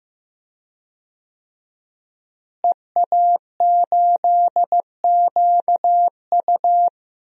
Morse code, heard as a keyed tone, about 15 wpm, 700 Hz. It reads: EA8QU
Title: DXCC-100 Call Signs - 5 Characters - Repeated 2x - 15wpm